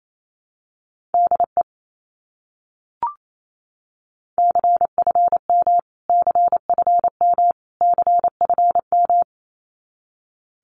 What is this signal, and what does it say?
Morse code, heard as a keyed tone, about 28 wpm, 700 Hz. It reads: DE CFM CFM CFM
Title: Abbreviations Repeated 3x - 28wpm